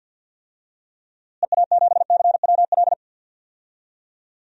Morse code, heard as a keyed tone, about 52 wpm, 700 Hz. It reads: EA7CWL
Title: DXCC-100 Call Signs - 6 Characters - Repeated 3x - 50wpm